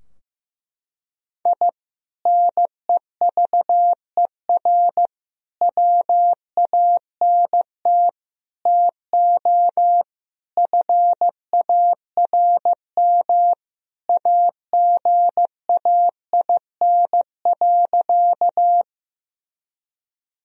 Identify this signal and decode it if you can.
Morse code, heard as a keyed tone, about 16 wpm, 700 Hz. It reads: I NEVER WANT TO FARM AGAIN.
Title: Sentences from Top 1000 Words - Review Part 2 - 15wpm